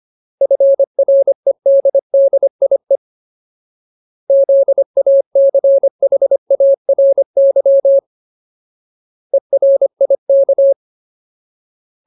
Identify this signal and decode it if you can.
Morse code, heard as a keyed tone, about 25 wpm, 550 Hz. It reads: FREDDIE ZACHARY ERIK
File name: Ultimate Word List / 25WPM Ultimate